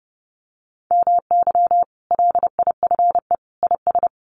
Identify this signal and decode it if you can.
Morse code, heard as a keyed tone, about 30 wpm, 700 Hz. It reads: MY LIFE IS
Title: Sentences from Top 1400 Words 30wpm